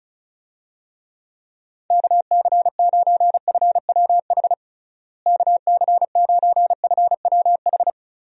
Morse code, heard as a keyed tone, about 36 wpm, 700 Hz. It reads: KC9FWH KC9FWH
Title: USA Call Signs - 6 Characters - Repeated 2x - 35wpm